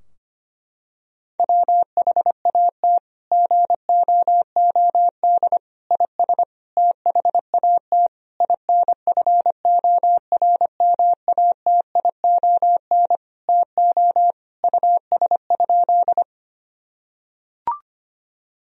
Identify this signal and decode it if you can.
Morse code, heard as a keyed tone, about 25 wpm, 700 Hz. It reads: WHAT GOOD IS THAT INFORMATION TO US?
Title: Sentences from Top 1000 Words - Review Part 3 - 25wpm